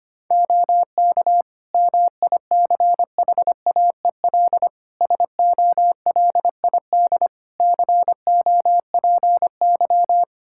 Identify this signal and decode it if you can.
Morse code, heard as a keyed tone, about 25 wpm, 700 Hz. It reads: OK MICHAEL SOLID COPY
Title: Practice QSO - Exchange 3 - More Info - 25wpm